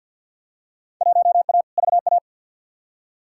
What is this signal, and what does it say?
Morse code, heard as a keyed tone, about 50 wpm, 700 Hz. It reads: 1A VA
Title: Field Day Exchanges Encore - 50wpm